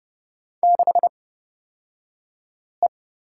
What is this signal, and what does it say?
Morse code, heard as a keyed tone, about 30 wpm, spaced as for 5 wpm, 700 Hz. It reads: 6E
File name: Lesson 38d - Letter q - Two Characters 30wpm